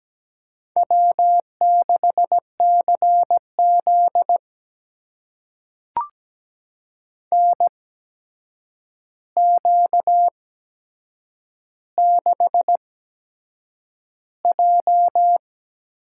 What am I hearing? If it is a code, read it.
Morse code, heard as a keyed tone, about 17 wpm, spaced as for 7 wpm, 700 Hz. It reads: W6CZ NQ6J
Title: USA Call Signs - 4 Characters - 8x Character Spacing - 17wpm